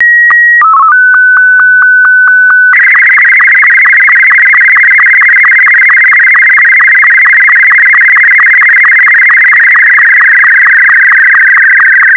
SSTV (slow-scan television), Martin 2.
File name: SSTV Martin2